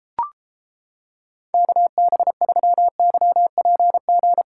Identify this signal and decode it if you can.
Morse code, heard as a keyed tone, about 31 wpm, 700 Hz. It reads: KB3YPG